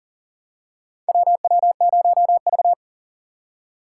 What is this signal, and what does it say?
Morse code, heard as a keyed tone, about 40 wpm, 700 Hz. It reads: WW0V
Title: USA Call Signs - 2x Character Spacing - 40wpm